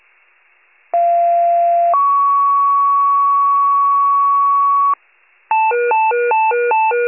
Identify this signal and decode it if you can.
Two-tone page, heard as a one-second tone, then a three-second tone, about 690 then 1120 Hz, with noise. It is heard from 0.9 to 4.9 s.
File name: wcalert